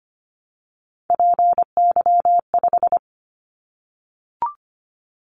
Morse code, heard as a keyed tone, about 25 wpm, 700 Hz. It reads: PY5
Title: Callsign Prefixes for Top 50 DXCC - 25wpm